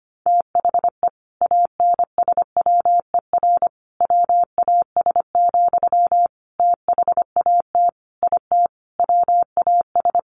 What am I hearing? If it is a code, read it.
Morse code, heard as a keyed tone, about 25 wpm, 700 Hz. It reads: THE ANSWER WAS, THAT IT WAS
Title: Sentences from Top 1100 Words 25wpm